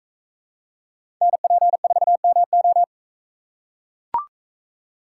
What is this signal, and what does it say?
Morse code, heard as a keyed tone, about 42 wpm, 700 Hz. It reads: NP4MO